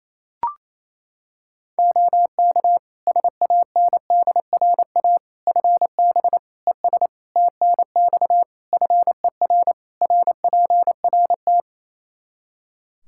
Morse code, heard as a keyed tone, about 28 wpm, 700 Hz. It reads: OK SANDRA FB ES TNX FER RPRT
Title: Practice QSO - Exchange 2 - Info - 28wpm